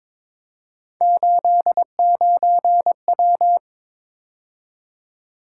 Morse code, heard as a keyed tone, about 22 wpm, 700 Hz. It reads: 89W
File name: practice / 22wpm / Three Letter-Number Combinations - 22wpm